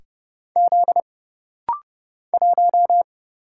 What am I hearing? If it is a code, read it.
Morse code, heard as a keyed tone, about 30 wpm, 700 Hz. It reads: Z 1
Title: Single Letter-Number - Rapid-Fire 30wpm